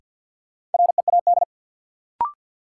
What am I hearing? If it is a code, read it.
Morse code, heard as a keyed tone, about 53 wpm, 700 Hz. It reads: READ
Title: Sentences from Top 200 Words - New Word Drill - 50wpm